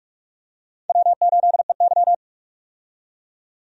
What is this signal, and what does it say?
Morse code, heard as a keyed tone, about 43 wpm, 700 Hz. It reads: W8EY
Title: USA Call Signs - 4 Characters - 45wpm